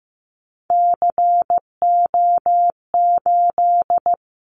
Morse code, heard as a keyed tone, about 15 wpm, 700 Hz. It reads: CO8
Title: Callsign Prefixes for Top 100 DXCC - 15wpm